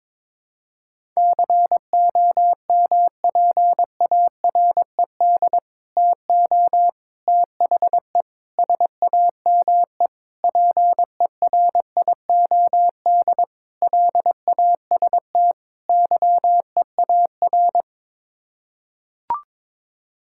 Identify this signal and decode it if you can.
Morse code, heard as a keyed tone, about 21 wpm, 700 Hz. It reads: COMPARED TO THE SAME PERIOD LAST YEAR
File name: practice / 22wpm / Sets of 7 Words - 22wpm